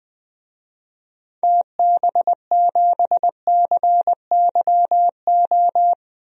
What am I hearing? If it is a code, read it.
Morse code, heard as a keyed tone, about 21 wpm, 700 Hz. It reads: TB7CYO
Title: DXCC-100 Call Signs - 6 Characters - Repeated 3x - 20wpm